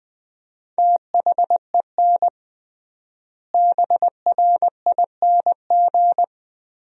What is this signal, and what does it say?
Morse code, heard as a keyed tone, about 20 wpm, 700 Hz. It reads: THEN BRING
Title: 3x Word-Spacing - Sentences from Top 500 Words - 20wpm